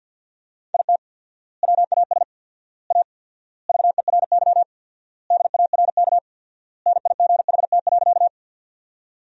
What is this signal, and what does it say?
Morse code, heard as a keyed tone, about 50 wpm, 700 Hz. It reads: IT WAS A VERY DARK NIGHT.
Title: 4x Word-Spacing - Sentences from Top 500 Words - 50wpm